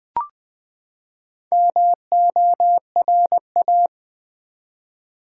Morse code, heard as a keyed tone, about 20 wpm, 700 Hz. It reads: MORA
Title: Build A Word - 4 Letters At A Time - 8-Letter Words - 20wpm